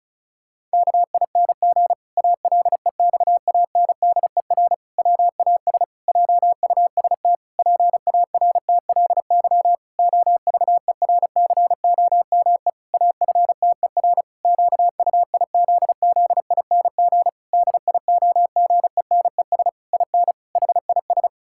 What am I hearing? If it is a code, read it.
Morse code, heard as a keyed tone, about 34 wpm, 700 Hz. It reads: KING ALEXANDER WAS JUST PARTLY OVERCOME AFTER QUIZZING DIOGENES IN HIS
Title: Pangram Challenge v2 - 35wpm